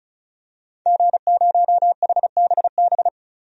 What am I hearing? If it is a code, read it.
Morse code, heard as a keyed tone, about 34 wpm, 700 Hz. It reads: G0HBB
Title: DXCC-100 Call Signs - 5 Characters - 35wpm